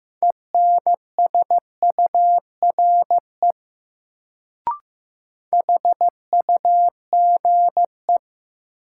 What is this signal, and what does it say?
Morse code, heard as a keyed tone, about 15 wpm, 700 Hz. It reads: ENSURE HUGE
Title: Sentences from Top 1100 Words - New Word Drill - 15wpm